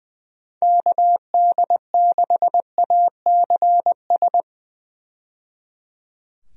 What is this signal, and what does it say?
Morse code, heard as a keyed tone, about 20 wpm, 700 Hz. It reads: KD6ACS
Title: USA Call Signs - 6 Characters - 20wpm